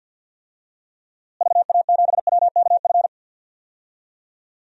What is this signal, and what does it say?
Morse code, heard as a keyed tone, about 53 wpm, 700 Hz. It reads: VA7WKF
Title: DXCC-100 Call Signs - 6 Characters - Repeated 3x - 50wpm